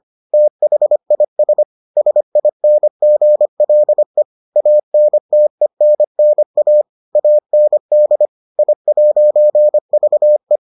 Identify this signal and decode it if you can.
Morse code, heard as a keyed tone, about 25 wpm, 600 Hz. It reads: THIS SINGLE ANTENNA AND I'VE